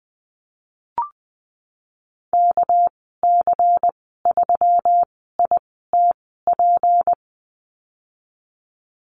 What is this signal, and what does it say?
Morse code, heard as a keyed tone, about 19 wpm, spaced as for 14 wpm, 700 Hz. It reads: KC3ITP
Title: 207 Lesson 33g - Letter j - Call Signs 20wpm